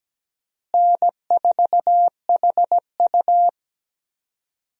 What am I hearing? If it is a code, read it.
Morse code, heard as a keyed tone, about 17 wpm, 700 Hz. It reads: N4HU
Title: USA Call Signs - 4 Characters - Repeated 3x - 17wpm